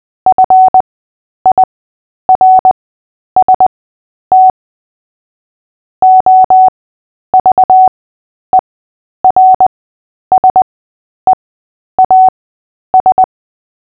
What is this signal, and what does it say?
Morse code, heard as a keyed tone, about 20 wpm, spaced as for 10 wpm, 750 Hz. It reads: FIRST OVERSEAS
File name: cw / fists / Short-Story-3-10wpm